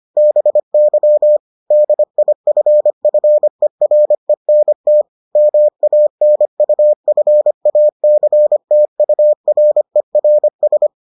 Morse code, heard as a keyed tone, about 25 wpm, 600 Hz. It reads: BY DIFFERENT MANUFACTURERS